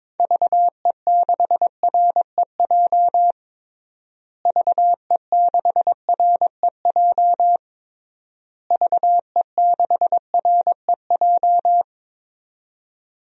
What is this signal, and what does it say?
Morse code, heard as a keyed tone, about 22 wpm, 700 Hz. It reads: VE6REJ VE6REJ VE6REJ
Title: DXCC-100 Call Signs - 6 Characters - Repeated 3x - 22wpm